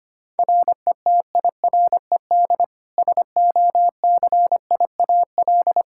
Morse code, heard as a keyed tone, about 25 wpm, 700 Hz. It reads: RETIRED SOCIAL
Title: K1USN SST QSO Practice 25wpm